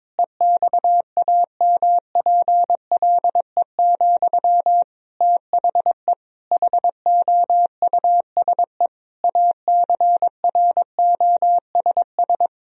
Morse code, heard as a keyed tone, about 22 wpm, 700 Hz. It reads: EXAMPLE, THE HOUSE ACROSS
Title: Sentences from Top 1000 Words - Review Part 1 - 22wpm